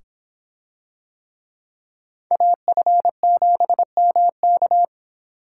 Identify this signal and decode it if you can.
Morse code, heard as a keyed tone, about 24 wpm, 700 Hz. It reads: AF7MK